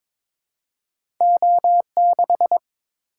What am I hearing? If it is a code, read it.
Morse code, heard as a keyed tone, about 22 wpm, 700 Hz. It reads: O6
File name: Two Letter-Number Combinations - 22wpm